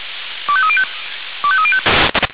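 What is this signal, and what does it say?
Five-tone selective call (ZVEI), heard as quick runs of five tones, tones of about 70 ms. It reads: 25305 25305
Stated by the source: Five-tone ZVEI selective call received on an SDR.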